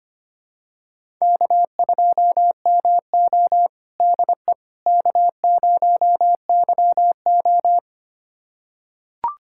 Morse code, heard as a keyed tone, about 25 wpm, 700 Hz. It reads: K2MO DE K0YO
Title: USA Callsign DE Callsign - 4 Characters - 25wpm